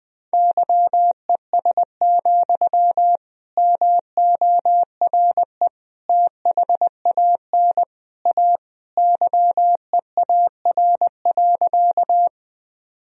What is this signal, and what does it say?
Morse code, heard as a keyed tone, about 19 wpm, 700 Hz. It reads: YES, MORE THAN A YEAR.
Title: Sentences from Top 1000 Words - Review Part 1 - 20wpm